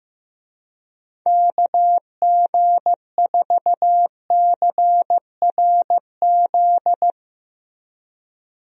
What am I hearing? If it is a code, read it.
Morse code, heard as a keyed tone, about 16 wpm, 700 Hz. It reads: KG4CRZ
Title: USA Call Signs - 6 Characters - Repeated 2x - 15wpm